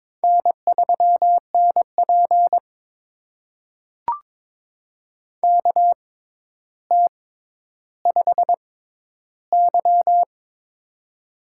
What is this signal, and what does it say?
Morse code, heard as a keyed tone, about 22 wpm, spaced as for 11 wpm, 700 Hz. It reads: N3NP KT5Y
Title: USA Call Signs - 4 Characters - 6x Character Spacing - 22wpm